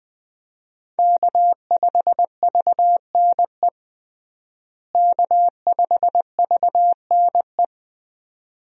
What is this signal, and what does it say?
Morse code, heard as a keyed tone, about 20 wpm, 700 Hz. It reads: K5VNE K5VNE
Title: USA Call Signs - 5 Characters - Repeated 2x - 20wpm